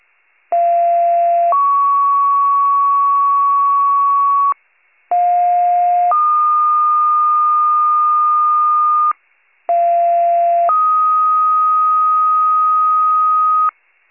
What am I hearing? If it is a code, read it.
Two-tone page, heard as a one-second tone, then a three-second tone, about 690 then 1120 Hz, about 710 then 1220 Hz, about 690 then 1250 Hz. It is heard from 0.5 to 13.7 s.